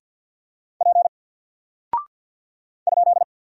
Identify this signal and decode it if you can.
Morse code, heard as a keyed tone, about 48 wpm, 700 Hz. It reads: P ?
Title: Single Character (Letter:Number:Punctuation:Prosign) Alt 50wpm